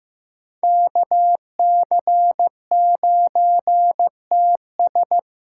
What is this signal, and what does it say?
Morse code, heard as a keyed tone, about 18 wpm, 700 Hz. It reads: KC9TS
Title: Speed Racing - USA 2x2 Call Signs 15wpm